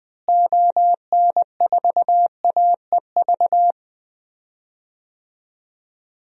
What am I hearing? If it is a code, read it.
Morse code, heard as a keyed tone, about 20 wpm, 700 Hz. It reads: ON4AEV
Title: DXCC-100 Call Signs - 6 Characters - Repeated 3x - 20wpm